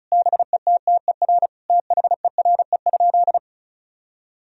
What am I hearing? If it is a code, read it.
Morse code, heard as a keyed tone, about 34 wpm, 700 Hz. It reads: BETTER THERE?
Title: Sentences from Top 1000 Words - Review Part 4 - 35wpm